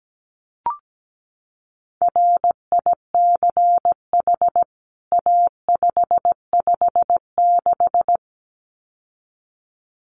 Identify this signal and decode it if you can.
Morse code, heard as a keyed tone, about 17 wpm, 700 Hz. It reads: RICH A556
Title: CWT Exchange 17wpm